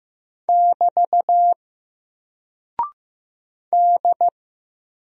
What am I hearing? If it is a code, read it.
Morse code, heard as a keyed tone, about 15 wpm, 700 Hz. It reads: <BT> D
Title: Single Character (Letter:Number:Punctuation:Prosign) 15wpm